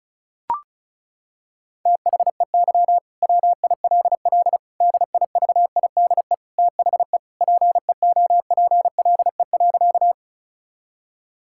Morse code, heard as a keyed tone, about 36 wpm, 700 Hz. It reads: THEY WILL DIVIDE THE PEOPLE.
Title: Sentences from Top 1600 Words 35wpm